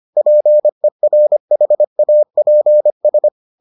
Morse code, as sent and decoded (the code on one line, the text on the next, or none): .--. .-. .... .- .--. ...
PERHAPS